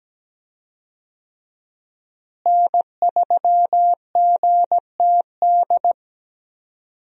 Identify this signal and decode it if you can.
Morse code, heard as a keyed tone, about 17 wpm, 700 Hz. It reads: N3GTD
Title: USA Call Signs - 5 Characters - Repeated 2x - 17wpm